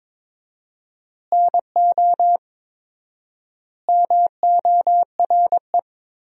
Morse code, heard as a keyed tone, about 23 wpm, 700 Hz. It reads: NO MORE
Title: 4x Word-Spacing - Sentences from Top 200 Words - 22wpm